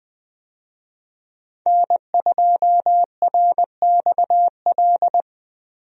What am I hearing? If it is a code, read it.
Morse code, heard as a keyed tone, about 20 wpm, 700 Hz. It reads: N2RXL
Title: USA Call Signs - 5 Characters - Repeated 2x - 20wpm